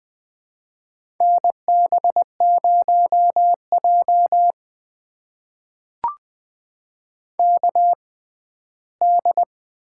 Morse code, heard as a keyed tone, about 20 wpm, spaced as for 14 wpm, 700 Hz. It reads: NB0J KD
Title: USA Call Signs - 6x Character Spacing - 20wpm